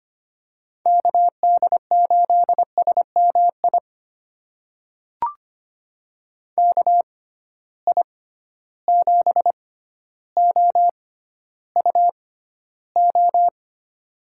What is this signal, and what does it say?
Morse code, heard as a keyed tone, about 25 wpm, spaced as for 13 wpm, 700 Hz. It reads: KD8SMI KI7OUO
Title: USA Call Signs - 6 Characters - 6x Character Spacing - 25wpm